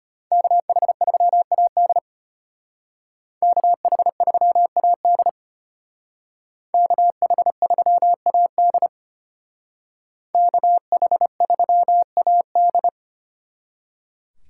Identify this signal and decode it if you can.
Morse code, heard as a keyed tone, about 32 wpm, 700 Hz. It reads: KH3AD KH3AD KH3AD KH3AD